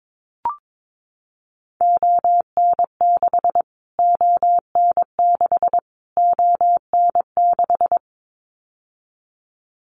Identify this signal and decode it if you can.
Morse code, heard as a keyed tone, about 22 wpm, 700 Hz. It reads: ON6 ON6 ON6
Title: Callsign Prefixes for Top 25 DXCC - Repeated 3x - 22wpm